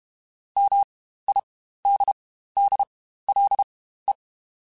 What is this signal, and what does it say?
Morse code, heard as a keyed tone, about 32 wpm, spaced as for 15 wpm, 800 Hz. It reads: MIDDLE
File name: Spaced / Chuckles Spaced-mp3s / MiddleAge sp @30